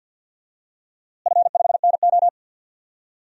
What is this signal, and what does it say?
Morse code, heard as a keyed tone, about 49 wpm, 700 Hz. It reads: F5NO
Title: DXCC-100 Call Signs - 4 Characters - Repeated 2x - 50wpm